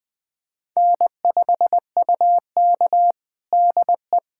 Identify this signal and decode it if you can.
Morse code, heard as a keyed tone, about 20 wpm, 700 Hz. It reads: N5UK DE
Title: USA Callsign DE Callsign - 4 Characters - 20wpm